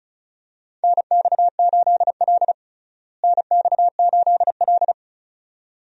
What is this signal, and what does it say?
Morse code, heard as a keyed tone, about 36 wpm, 700 Hz. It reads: NX8L NX8L
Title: USA Call Signs - 4 Characters - Repeated 2x - 35wpm